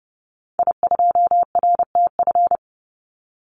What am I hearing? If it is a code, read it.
Morse code, heard as a keyed tone, about 30 wpm, 700 Hz. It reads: I2RTF
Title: Call Signs - 30wpm